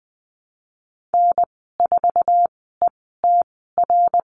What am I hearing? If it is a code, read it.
Morse code, heard as a keyed tone, about 19 wpm, spaced as for 14 wpm, 700 Hz. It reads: N4ETR
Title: 076 Lesson 13g - Letter L - Call Signs 20wpm